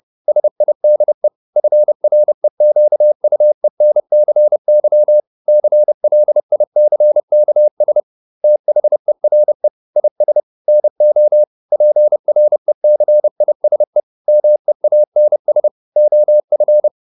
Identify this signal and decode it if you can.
Morse code, heard as a keyed tone, about 30 wpm, 600 Hz. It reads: SIDE FREQUENCY CLICKS THERE IS NO PRECISE MEANS OF